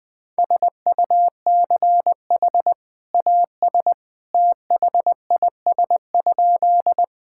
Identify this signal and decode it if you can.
Morse code, heard as a keyed tone, about 20 wpm, 700 Hz. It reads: SUCH AS THIS?
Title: Sentences with 2-Word Phrases - 20wpm